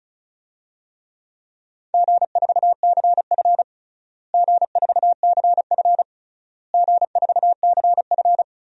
Morse code, heard as a keyed tone, about 36 wpm, 700 Hz. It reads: G4CF G4CF G4CF
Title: DXCC-100 Call Signs - 4 Characters - Repeated 3x - 35wpm